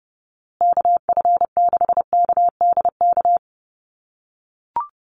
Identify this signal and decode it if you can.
Morse code, heard as a keyed tone, about 30 wpm, 700 Hz. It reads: KF6KDK
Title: USA Call Signs Repeated 3x 30wpm